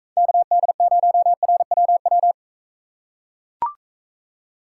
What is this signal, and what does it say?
Morse code, heard as a keyed tone, about 40 wpm, 700 Hz. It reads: KD0RWW